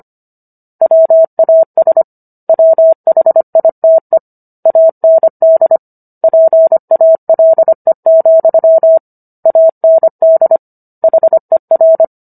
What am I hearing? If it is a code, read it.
Morse code, heard as a keyed tone, about 25 wpm, 650 Hz. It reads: WAS WHITE AND PALE, AND HER